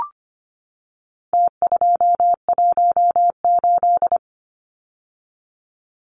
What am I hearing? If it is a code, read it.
Morse code, heard as a keyed tone, about 25 wpm, 700 Hz. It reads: T218
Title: Numbers - Contest Style - 4-Digit 25wpm